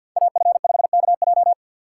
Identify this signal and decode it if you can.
Morse code, heard as a keyed tone, about 51 wpm, 700 Hz. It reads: AF5KJ
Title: USA Call Signs - 5 Characters - 3x Character Spacing - 50wpm